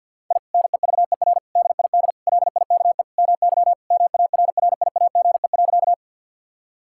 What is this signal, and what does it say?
Morse code, heard as a keyed tone, about 49 wpm, 700 Hz. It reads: I NEVER DID LIKE MY MARRIAGE.
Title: Sentences from Top 1000 Words - Review Part 5 - 50wpm